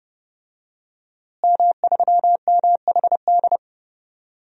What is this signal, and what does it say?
Morse code, heard as a keyed tone, about 32 wpm, 700 Hz. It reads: M3MHD
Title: DXCC-100 Call Signs - 5 Characters - Repeated 3x - 30wpm